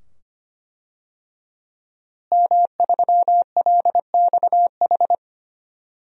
Morse code, heard as a keyed tone, about 25 wpm, 700 Hz. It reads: M3LXH
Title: DXCC-100 Call Signs - 5 Characters - Repeated 2x - 25wpm